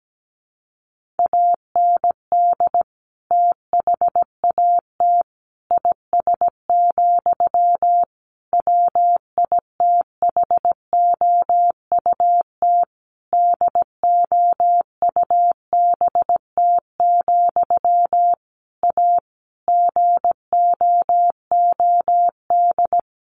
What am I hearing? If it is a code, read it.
Morse code, heard as a keyed tone, about 17 wpm, 700 Hz. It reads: AND THAT IS, WITHOUT DOUBT, A GOOD
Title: Sentences from Top 1500 Words 17wpm